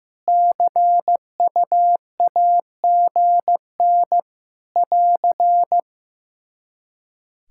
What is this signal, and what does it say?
Morse code, heard as a keyed tone, about 15 wpm, 700 Hz. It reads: CUAGN <AR>
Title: Practice QSO - Exchange 4 - Ending - 15wpm